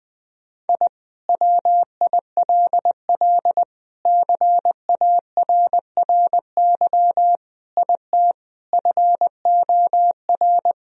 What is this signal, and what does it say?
Morse code, heard as a keyed tone, about 20 wpm, 700 Hz. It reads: I WILL CARRY IT FOR
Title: Sentences from Top 1000 Words - Review Part 1 - 20wpm